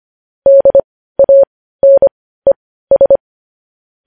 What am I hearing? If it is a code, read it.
Morse code, heard as a keyed tone, about 25 wpm, spaced as for 15 wpm, 550 Hz. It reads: DANES
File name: podcast / News headlines-1x-25x15